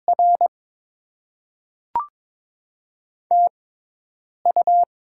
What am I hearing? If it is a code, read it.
Morse code, heard as a keyed tone, about 22 wpm, spaced as for 8 wpm, 700 Hz. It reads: R TU
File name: Top 20 QSO Elements - 6x Character Spacing - 22wpm